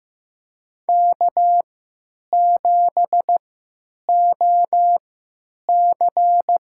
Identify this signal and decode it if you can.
Morse code, heard as a keyed tone, about 15 wpm, spaced as for 9 wpm, 700 Hz. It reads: K7OC
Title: USA Call Signs - 4 Characters - 3x Character Spacing - 15wpm